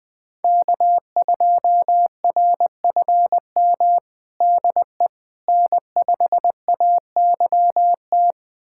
Morse code, heard as a keyed tone, about 20 wpm, 700 Hz. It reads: K2RFM DE N5AYT
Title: USA Callsign DE Callsign - 5 Characters - 20wpm